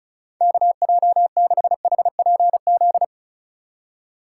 Morse code, heard as a keyed tone, about 34 wpm, 700 Hz. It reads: KJ6HPZ
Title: USA Call Signs - 6 Characters - 8x Character Spacing - 35wpm